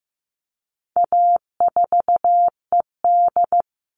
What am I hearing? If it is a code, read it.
Morse code, heard as a keyed tone, about 15 wpm, 700 Hz. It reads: A4ED
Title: All Cut Numbers - 4-Digit 15wpm